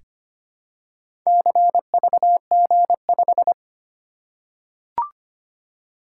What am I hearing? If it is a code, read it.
Morse code, heard as a keyed tone, about 25 wpm, 700 Hz. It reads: CVG5
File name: practice / 25wpm / Four Letter-Number Combinations - 25wpm